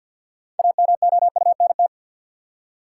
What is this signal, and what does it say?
Morse code, heard as a keyed tone, about 53 wpm, 700 Hz. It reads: AMOUNT
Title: Six Letter Words - 50wpm